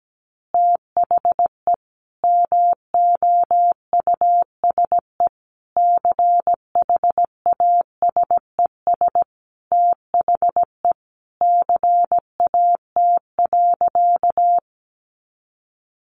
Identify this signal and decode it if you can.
Morse code, heard as a keyed tone, about 17 wpm, 700 Hz. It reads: THE MOUSE CHASES THE CAT.
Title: Easy Phrases 17wpm